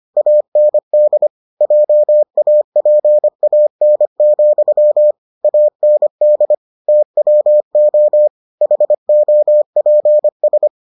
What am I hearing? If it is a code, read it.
Morse code, heard as a keyed tone, about 25 wpm, 600 Hz. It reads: AND JAPAN, AND TWO HOPS